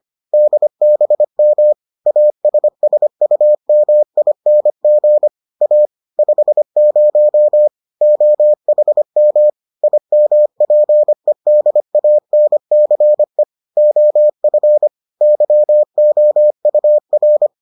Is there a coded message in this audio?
Yes: Morse code from 0.3 to 17.5 s, heard as a keyed tone, about 25 wpm, 600 Hz. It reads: DBM ASSUMING A 50 OHM IMPEDANCE OF YOUR